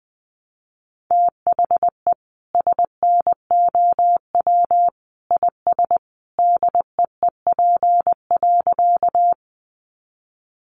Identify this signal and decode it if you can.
Morse code, heard as a keyed tone, about 20 wpm, 700 Hz. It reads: THE SNOW IS DEEP.
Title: Sentences from Top 500 Words 20wpm